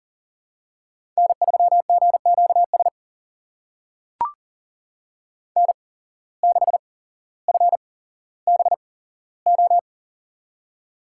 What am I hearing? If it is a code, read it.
Morse code, heard as a keyed tone, about 40 wpm, spaced as for 17 wpm, 700 Hz. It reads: N3GQS N6FBO
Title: USA Call Signs - 5 Characters - 8x Character Spacing - 40wpm